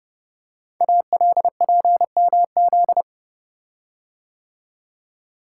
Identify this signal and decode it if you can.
Morse code, heard as a keyed tone, about 30 wpm, 700 Hz. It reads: ALPMZ
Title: Code Groups - 5 Letter:Numbers - 30wpm